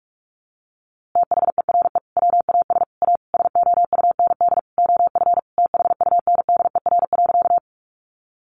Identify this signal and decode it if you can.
Morse code, heard as a keyed tone, about 46 wpm, 700 Hz. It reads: THERE WAS A SOUND OF THUNDER.
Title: A Sound Of Thunder 45wpm